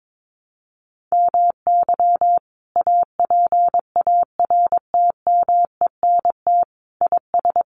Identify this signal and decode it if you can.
Morse code, heard as a keyed tone, about 22 wpm, 700 Hz. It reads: MY APARTMENT IS
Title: Sentences from Top 900 Words 22wpm